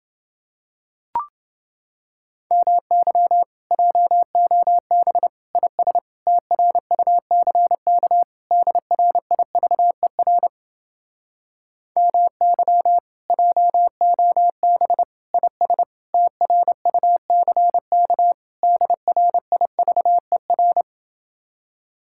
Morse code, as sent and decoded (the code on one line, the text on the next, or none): -- -.-- / .--- --- -... / .. ... / - .-. ..- -.-. -.- / -.. .-. .. ...- . .-. / -- -.-- / .--- --- -... / .. ... / - .-. ..- -.-. -.- / -.. .-. .. ...- . .-.
MY JOB IS TRUCK DRIVER MY JOB IS TRUCK DRIVER